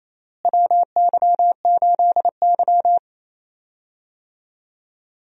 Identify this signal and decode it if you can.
Morse code, heard as a keyed tone, about 28 wpm, 700 Hz. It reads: WY8Y
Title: USA Call Signs - 4 Characters - Repeated 2x - 28wpm